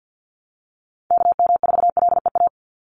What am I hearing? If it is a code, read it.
Morse code, heard as a keyed tone, about 50 wpm, 700 Hz. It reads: KM4LEA
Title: USA Call Signs 50wpm